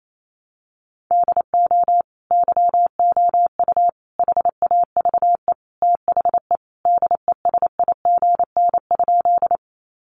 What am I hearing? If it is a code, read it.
Morse code, heard as a keyed tone, about 28 wpm, 700 Hz. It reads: DO YOU HAVE THE DESIGN?
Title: Sentences from Top 800 Words 28wpm